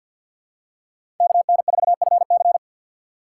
Morse code, heard as a keyed tone, about 53 wpm, 700 Hz. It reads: KN4FC
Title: USA Call Signs - 2x Character Spacing - 50wpm